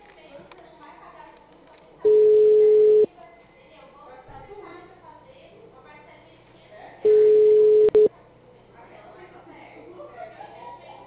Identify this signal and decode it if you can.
Ringback tone (Europe, 425 Hz), with speech.